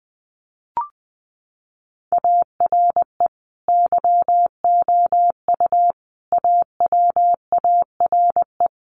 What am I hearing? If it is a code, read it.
Morse code, heard as a keyed tone, about 20 wpm, 700 Hz. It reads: ARE YOU AWARE
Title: Sentences from Top 1200 Words 20wpm